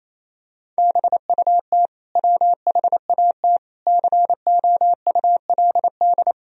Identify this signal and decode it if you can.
Morse code, heard as a keyed tone, about 28 wpm, 700 Hz. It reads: BUT WHAT COULD
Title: Sentences from Top 1000 Words - Review Part 3 - 28wpm